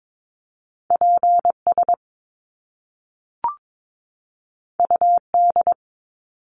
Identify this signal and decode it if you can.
Morse code, heard as a keyed tone, about 22 wpm, 700 Hz. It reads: PS UD
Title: Top Two Letter Combinations in Words 22wpm